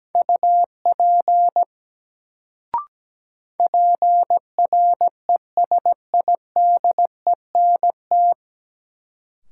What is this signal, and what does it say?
Morse code, heard as a keyed tone, about 17 wpm, 700 Hz. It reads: UP PRESIDENT
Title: Sets of 2 Words - New Word Drill - 17wpm